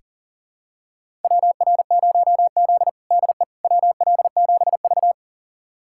Morse code, heard as a keyed tone, about 40 wpm, 700 Hz. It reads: WR0Z DE WL7V
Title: USA Callsign DE Callsign - 4 Characters - 40wpm